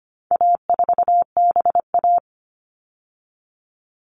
Morse code, heard as a keyed tone, about 25 wpm, 700 Hz. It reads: A4BA
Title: All Cut Numbers - 4-Digit 25wpm